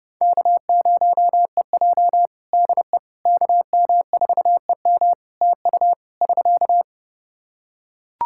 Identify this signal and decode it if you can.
Morse code, heard as a keyed tone, about 32 wpm, 700 Hz. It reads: K0EJ DE KM4EM TU <SK>
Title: Practice QSO - Exchange 4 - Ending - 30wpm